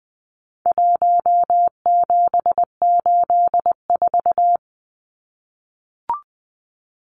Morse code, heard as a keyed tone, about 20 wpm, 700 Hz. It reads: 1784
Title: Numbers - 4-Digit 20wpm